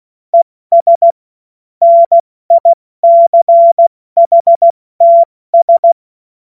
Morse code, heard as a keyed tone, about 16 wpm, spaced as for 14 wpm, 680 Hz. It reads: ES NICHTS
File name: Morsekurs / Witze / Witz 28.November 2023 70 F80